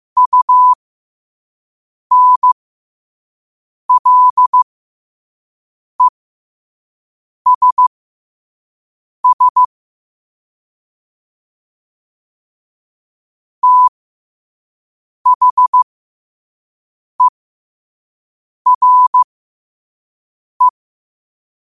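Morse code, heard as a keyed tone, about 15 wpm, spaced as for 5 wpm, 1000 Hz. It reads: UNLESS THERE
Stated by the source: Morse quote-of-the-day practice at 5 wpm.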